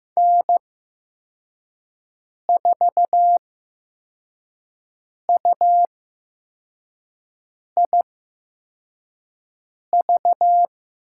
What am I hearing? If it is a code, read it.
Morse code, heard as a keyed tone, about 15 wpm, spaced as for 4 wpm, 700 Hz. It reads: N4UIV
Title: USA Call Signs - 5 Characters - 8x Character Spacing - 15wpm